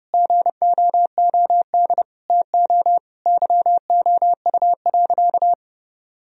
Morse code, heard as a keyed tone, about 30 wpm, 700 Hz. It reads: GOOD TO YOU.
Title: Sentences from Top 1000 Words - Review Part 1 - 30wpm